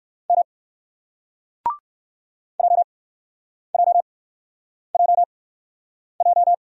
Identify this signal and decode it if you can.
Morse code, heard as a keyed tone, about 58 wpm, spaced as for 9 wpm, 700 Hz. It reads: M JJJJ